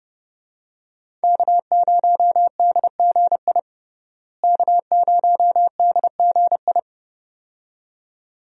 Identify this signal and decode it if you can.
Morse code, heard as a keyed tone, about 32 wpm, 700 Hz. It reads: K0DGI K0DGI
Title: USA Call Signs - 5 Characters - Repeated 2x - 30wpm